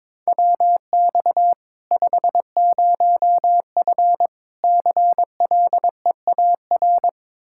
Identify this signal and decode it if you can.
Morse code, heard as a keyed tone, about 22 wpm, 700 Hz. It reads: WX 50F CLEAR
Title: Practice QSO - Exchange 2 - Info - 22wpm